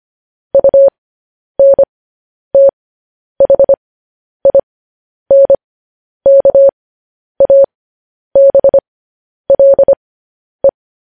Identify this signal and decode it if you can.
Morse code, heard as a keyed tone, about 25 wpm, spaced as for 10 wpm, 550 Hz. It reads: UNTHINKABLE
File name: podcast / News headlines-1x-25x10